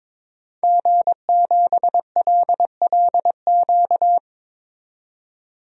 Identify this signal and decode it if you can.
Morse code, heard as a keyed tone, about 22 wpm, 700 Hz. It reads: G7LLQ
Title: DXCC-100 Call Signs - 5 Characters - 22wpm